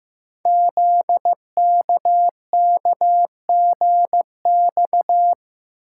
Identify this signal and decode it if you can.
Morse code, heard as a keyed tone, about 15 wpm, 700 Hz. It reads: ZKKGX
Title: Code Groups - 5 Letter:Numbers - 15wpm